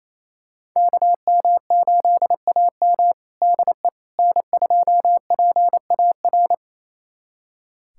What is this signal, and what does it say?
Morse code, heard as a keyed tone, about 28 wpm, 700 Hz. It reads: KM8AM DE N2PAR
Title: USA Callsign DE Callsign - 5 Characters - 28wpm